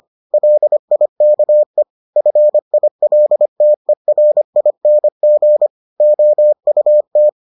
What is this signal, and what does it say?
Morse code, heard as a keyed tone, about 25 wpm, 600 Hz. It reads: LIKE FILTERING OUT